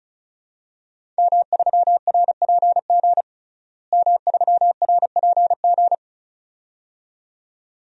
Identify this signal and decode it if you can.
Morse code, heard as a keyed tone, about 36 wpm, 700 Hz. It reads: M3RPG M3RPG
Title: DXCC-100 Call Signs - 5 Characters - Repeated 2x - 35wpm